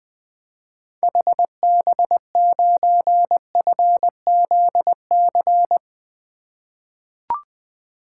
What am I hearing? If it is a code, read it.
Morse code, heard as a keyed tone, about 20 wpm, 700 Hz. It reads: HB9FZC
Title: 2023 SOTA Activator Call Signs - 20wpm